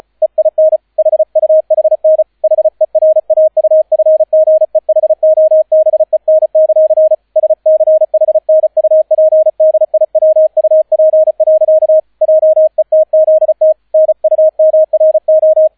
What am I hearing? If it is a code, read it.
Morse code, heard as a keyed tone, about 34 wpm, 630 Hz, with radio noise. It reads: EIN HUHN HERAUFGEHOBEN; SCHNUPDIWUP. JETZT NUMRO